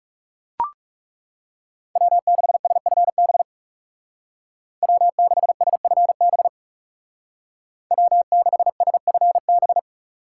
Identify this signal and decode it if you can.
Morse code, heard as a keyed tone, about 35 wpm, 700 Hz. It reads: W6SFB W6SFB W6SFB
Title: Speed Racing - USA 1x3 Call Signs 30wpm